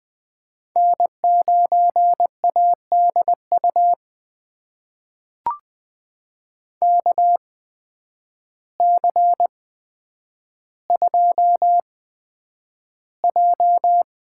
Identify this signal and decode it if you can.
Morse code, heard as a keyed tone, about 21 wpm, spaced as for 9 wpm, 700 Hz. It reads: N9ADU KC2J
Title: USA Call Signs - 8x Character Spacing - 20wpm